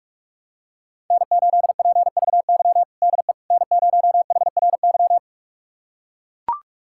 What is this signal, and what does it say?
Morse code, heard as a keyed tone, about 41 wpm, 700 Hz. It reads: N8PVY DE N0HRY
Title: USA Callsign DE Callsign - 5 Characters - 45wpm